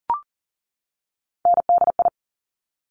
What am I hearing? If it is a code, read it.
Morse code, heard as a keyed tone, about 40 wpm, 700 Hz. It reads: NDI
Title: Top Three Letter Combinations in Words 40wpm